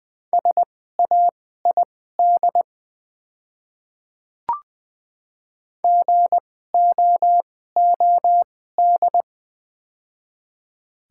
Morse code, heard as a keyed tone, about 20 wpm, spaced as for 14 wpm, 700 Hz. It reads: SAID GOOD
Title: 2x Character Spacing - Top 200 Words - 20wpm